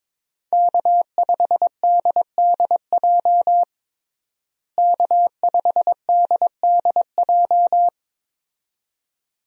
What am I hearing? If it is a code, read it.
Morse code, heard as a keyed tone, about 22 wpm, 700 Hz. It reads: K5DDJ K5DDJ
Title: USA Call Signs - 5 Characters - Repeated 2x - 22wpm